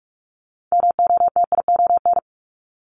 Morse code, heard as a keyed tone, about 46 wpm, 700 Hz. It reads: MOTION
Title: Multi-letter Phonemes 45wpm